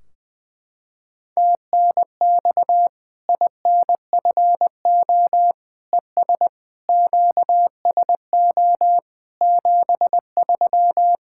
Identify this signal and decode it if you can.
Morse code, heard as a keyed tone, about 21 wpm, 700 Hz. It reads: TNX INFO ES QSO 73
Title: Practice QSO - Exchange 4 - Ending - 20wpm